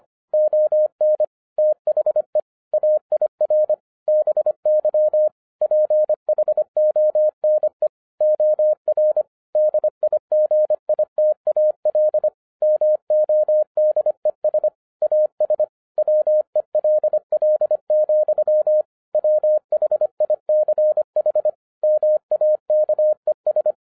Morse code, heard as a keyed tone, about 25 wpm, 610 Hz. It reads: ON THE AIR BY PHONE OR DIGITAL MODES AS WELL, WHICH MAKES